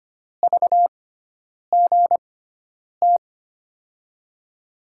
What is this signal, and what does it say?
Morse code, heard as a keyed tone, about 25 wpm, spaced as for 9 wpm, 700 Hz. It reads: VGT
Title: USA Call Signs - 6 Characters - 6x Character Spacing - 25wpm